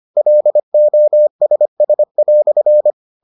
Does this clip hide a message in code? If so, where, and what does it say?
Yes: Morse code from 0.2 to 2.9 s, heard as a keyed tone, about 25 wpm, 600 Hz. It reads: LOSS"